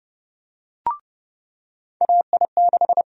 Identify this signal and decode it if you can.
Morse code, heard as a keyed tone, about 30 wpm, 700 Hz. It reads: AI6
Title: USA Call Sign Prefixes - 30wpm